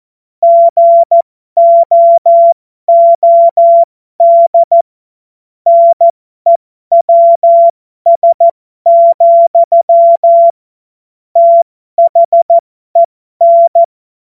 Morse code, heard as a keyed tone, about 14 wpm, spaced as for 12 wpm, 680 Hz. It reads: GOOD NEWS, THEN
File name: Morsekurs / Witze